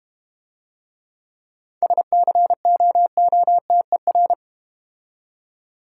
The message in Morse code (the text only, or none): SCOOTER